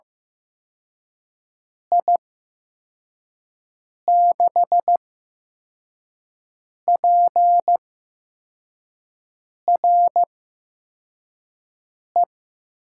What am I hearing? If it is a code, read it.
Morse code, heard as a keyed tone, about 15 wpm, spaced as for 4 wpm, 700 Hz. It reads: I6PRE
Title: USA Call Signs - 6 Characters - 8x Character Spacing - 15wpm